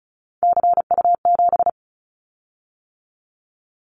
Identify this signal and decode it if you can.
Morse code, heard as a keyed tone, about 36 wpm, 700 Hz. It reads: CU7
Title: Callsign Prefixes for Top 100 DXCC - 35wpm